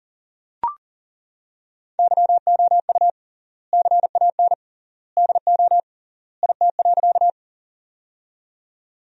Morse code, heard as a keyed tone, about 40 wpm, 700 Hz. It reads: YOU CAN DO IT.
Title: 3x Word-Spacing - Sentences from Top 100 Words - 40wpm